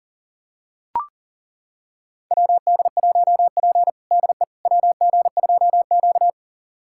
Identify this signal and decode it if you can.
Morse code, heard as a keyed tone, about 40 wpm, 700 Hz. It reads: WD1P DE WG2Q
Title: USA Callsign DE Callsign - 4 Characters - 40wpm